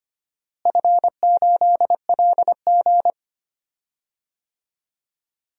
Morse code, heard as a keyed tone, about 25 wpm, 700 Hz. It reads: F8LG
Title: DXCC-100 Call Signs - 4 Characters - Repeated 2x - 25wpm